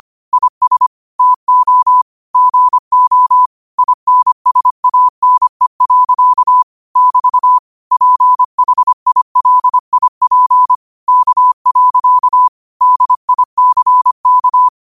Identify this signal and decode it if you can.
Morse code, heard as a keyed tone, about 25 wpm, 1000 Hz. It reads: IS TO GO INSANE. <BT> PHILIP K. DICK
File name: qotdfiles / 25wpm